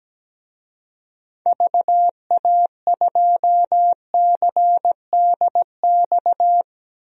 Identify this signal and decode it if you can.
Morse code, heard as a keyed tone, about 17 wpm, 700 Hz. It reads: VA2CDX
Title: DXCC-100 Call Signs - 6 Characters - 17wpm